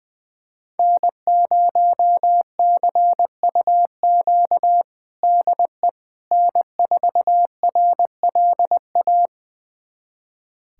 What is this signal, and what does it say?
Morse code, heard as a keyed tone, about 21 wpm, 700 Hz. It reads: N0CUQ DE N4RLA
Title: USA Callsign DE Callsign - 5 Characters - 20wpm